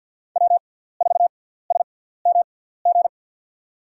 Morse code, heard as a keyed tone, about 49 wpm, spaced as for 17 wpm, 700 Hz. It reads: W4SMG
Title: USA Call Signs - 5 Characters - 6x Character Spacing - 50wpm